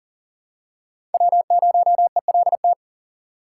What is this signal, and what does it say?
Morse code, heard as a keyed tone, about 40 wpm, 700 Hz. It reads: W0ELT
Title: USA Call Signs - 3x Character Spacing - 40wpm